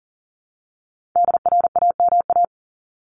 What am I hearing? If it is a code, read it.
Morse code, heard as a keyed tone, about 40 wpm, 700 Hz. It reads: DRAMA
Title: Top 1251-1500 Words 40wpm